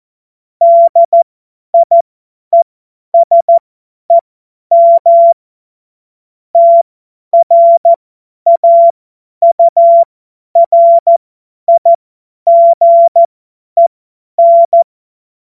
Morse code, heard as a keyed tone, about 14 wpm, spaced as for 10 wpm, 680 Hz. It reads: DIESEM TRAURIGEN
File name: Morsekurs / Witze / Witz 14.November 2023 50 F70